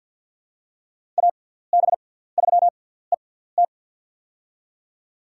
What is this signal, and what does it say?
Morse code, heard as a keyed tone, about 46 wpm, spaced as for 17 wpm, 700 Hz. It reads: AB3ET
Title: USA Call Signs - 5 Characters - 6x Character Spacing - 50wpm